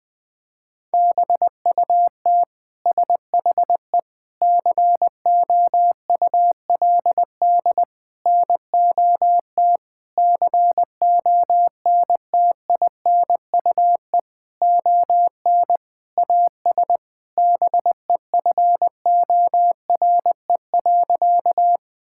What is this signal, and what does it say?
Morse code, heard as a keyed tone, about 21 wpm, 700 Hz. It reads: BUT SHE COULD NOT CONTINUE ON AS BEFORE.
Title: Sentences from Top 1000 Words - Review Part 6 - 20wpm